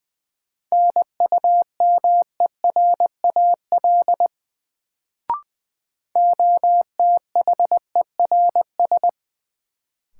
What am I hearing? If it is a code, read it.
Morse code, heard as a keyed tone, about 20 wpm, 700 Hz. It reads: NUMERAL OTHERS
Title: Sentences from Top 1000 Words - Review Part 2 - New Word Drill - 20wpm